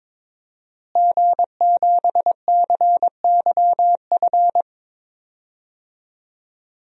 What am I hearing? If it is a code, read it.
Morse code, heard as a keyed tone, about 22 wpm, 700 Hz. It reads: G7CYF
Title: DXCC-100 Call Signs - 5 Characters - Repeated 2x - 22wpm